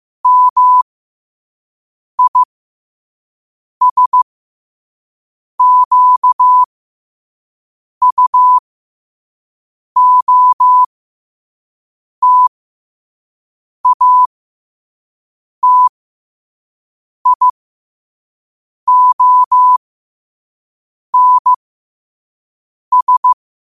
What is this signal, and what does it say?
Morse code, heard as a keyed tone, about 15 wpm, spaced as for 5 wpm, 1000 Hz. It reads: MISQUOTATIONS
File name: qotdfiles / 5wpm